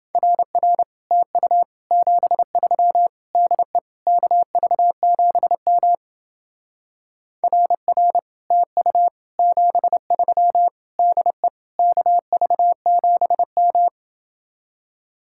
Morse code, heard as a keyed tone, about 29 wpm, 700 Hz. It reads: RR TU 73 DE KV7M RR TU 73 DE KV7M